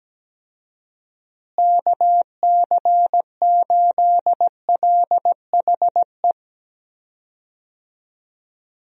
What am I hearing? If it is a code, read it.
Morse code, heard as a keyed tone, about 17 wpm, 700 Hz. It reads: KC8LHE
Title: USA Call Signs - 6 Characters - Repeated 2x - 17wpm